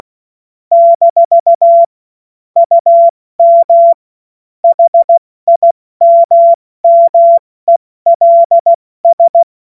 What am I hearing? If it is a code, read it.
Morse code, heard as a keyed tone, about 16 wpm, 680 Hz. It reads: - UM HIMMELS
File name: Morsekurs / Witze / Witz 07. Februar 2023 70 F80